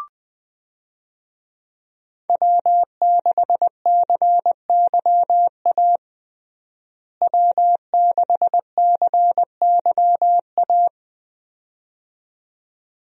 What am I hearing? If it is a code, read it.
Morse code, heard as a keyed tone, about 20 wpm, 700 Hz. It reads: W6CYA W6CYA
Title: USA Call Signs - 5 Characters - Repeated 2x - 20wpm